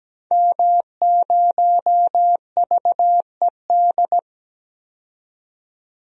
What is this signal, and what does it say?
Morse code, heard as a keyed tone, about 17 wpm, 700 Hz. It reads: M0VED
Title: DXCC-100 Call Signs - 5 Characters - 17wpm